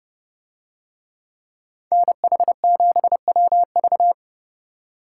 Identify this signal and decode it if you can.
Morse code, heard as a keyed tone, about 31 wpm, 700 Hz. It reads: NH7WV